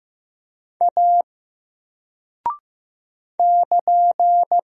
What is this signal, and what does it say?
Morse code, heard as a keyed tone, about 15 wpm, 700 Hz. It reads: A <KN>
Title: Single Character (Letter:Number:Punctuation:Prosign) 15wpm